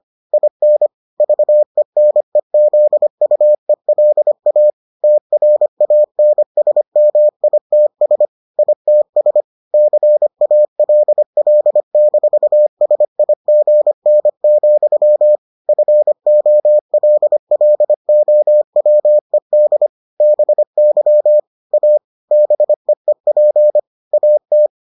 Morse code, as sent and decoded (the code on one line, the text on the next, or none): .. -. / ...- . -. --.. ..- . .-.. .- / - .-. .- -. ... -- .. - ... / .. - ... / -.-. .- .-.. .-.. -....- ... .. --. -. --..-- / ..-. --- .-.. .-.. --- .-- . -.. / -... -.-- / .- / -... . .--. / .- -
IN VENEZUELA TRANSMITS ITS CALL-SIGN, FOLLOWED BY A BEEP AT